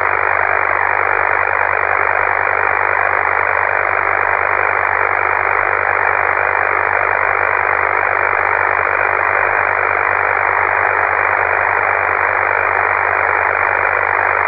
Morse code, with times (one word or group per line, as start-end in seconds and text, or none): none